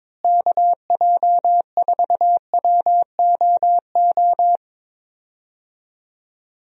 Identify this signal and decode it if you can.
Morse code, heard as a keyed tone, about 22 wpm, 700 Hz. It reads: KJ4WOO
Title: USA Call Signs - 6 Characters - Repeated 3x - 22wpm